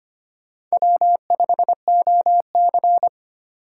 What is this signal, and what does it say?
Morse code, heard as a keyed tone, about 32 wpm, 700 Hz. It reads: W5OC